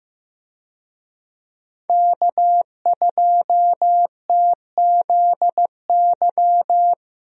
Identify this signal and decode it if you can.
Morse code, heard as a keyed tone, about 15 wpm, 700 Hz. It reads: K2TZY
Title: USA Call Signs - 5 Characters - Repeated 3x - 15wpm